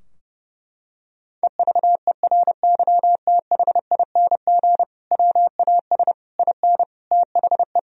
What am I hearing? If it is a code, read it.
Morse code, heard as a keyed tone, about 30 wpm, 700 Hz. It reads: EVERYTHING WAS IN THE
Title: Sentences from Top 1000 Words - Review Part 2 - 30wpm